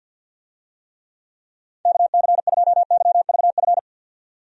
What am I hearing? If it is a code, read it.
Morse code, heard as a keyed tone, about 46 wpm, 700 Hz. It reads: KC2YVF
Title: USA Call Signs - 6 Characters - 50wpm